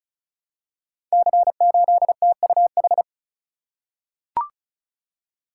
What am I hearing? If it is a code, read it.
Morse code, heard as a keyed tone, about 36 wpm, 690 Hz. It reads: C8TUH
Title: Code Groups - 5 Letter:Numbers - 35wpm